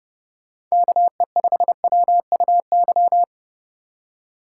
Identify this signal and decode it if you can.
Morse code, heard as a keyed tone, about 36 wpm, 700 Hz. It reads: KE5WUY
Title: Speed Racing - USA 2x3 Call Signs 30wpm